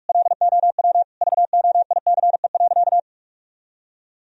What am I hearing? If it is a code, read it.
Morse code, heard as a keyed tone, about 46 wpm, 700 Hz. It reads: LOW VOICE.
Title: Sentences from Top 1000 Words - Review Part 1 - 45wpm